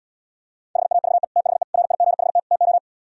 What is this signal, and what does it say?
Morse code, heard as a keyed tone, about 77 wpm, 700 Hz. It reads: HAVE THE RIGHT TO